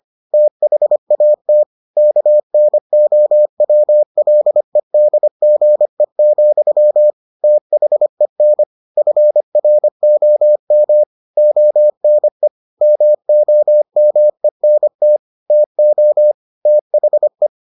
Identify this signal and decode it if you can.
Morse code, heard as a keyed tone, about 25 wpm, 600 Hz. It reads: THAT KNOWLEDGE, THEN FROM ONE MOMENT TO THE